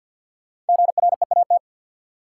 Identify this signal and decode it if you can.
Morse code, heard as a keyed tone, about 53 wpm, 700 Hz. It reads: GREAT
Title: Sentences from Top 200 Words - New Word Drill - 50wpm